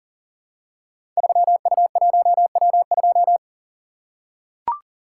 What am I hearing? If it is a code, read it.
Morse code, heard as a keyed tone, about 40 wpm, 700 Hz. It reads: 3U1W2
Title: Code Groups - 5 Letter:Numbers - 40wpm